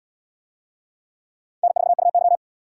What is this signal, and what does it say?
Morse code, heard as a keyed tone, about 69 wpm, 690 Hz. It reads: N4UY